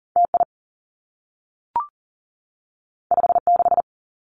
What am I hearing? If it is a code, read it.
Morse code, heard as a keyed tone, about 40 wpm, 700 Hz. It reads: T I 5 6
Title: Two Challenging Letter-Number Combinations 40wpm